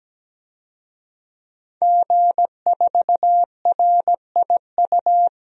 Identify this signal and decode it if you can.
Morse code, heard as a keyed tone, about 17 wpm, 700 Hz. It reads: G4RIU
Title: DXCC-100 Call Signs - 5 Characters - 17wpm